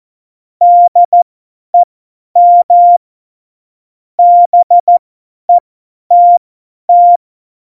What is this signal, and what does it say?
Morse code, heard as a keyed tone, about 14 wpm, spaced as for 10 wpm, 700 Hz. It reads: DEM BETT
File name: Morsekurs / Witze / Witz 29. September 2020, 50 F70